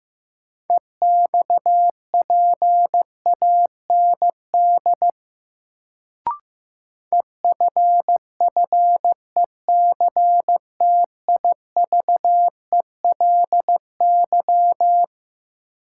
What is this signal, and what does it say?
Morse code, heard as a keyed tone, about 16 wpm, 700 Hz. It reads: EXPAND EFFECTIVELY
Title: Sentences from Top 1400 Words - New Word Drill Alt - 15wpm